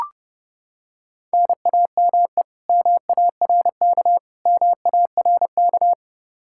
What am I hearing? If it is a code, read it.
Morse code, heard as a keyed tone, about 32 wpm, 700 Hz. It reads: NAME MARK MARK
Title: Practice QSO - Exchange 1 - Beginning - 30wpm